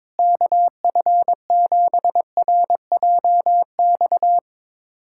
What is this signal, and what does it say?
Morse code, heard as a keyed tone, about 22 wpm, 700 Hz. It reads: KF7RJX
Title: USA Callsign DE Callsign - 6 Characters - 22wpm